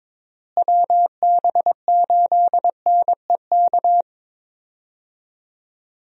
Morse code, heard as a keyed tone, about 22 wpm, 700 Hz. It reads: WB8NEK
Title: USA Call Signs - 6 Characters - Repeated 2x - 22wpm